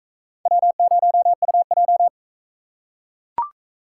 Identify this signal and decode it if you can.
Morse code, heard as a keyed tone, about 42 wpm, 700 Hz. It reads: W0UJ